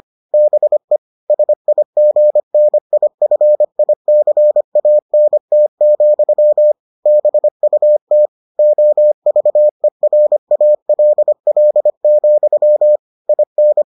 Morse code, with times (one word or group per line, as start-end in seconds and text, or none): 0.3-1.0 BE
1.3-6.7 SIGNIFICANT,
7.1-8.3 BUT
8.6-13.0 OVERALL,
13.3-13.8 IN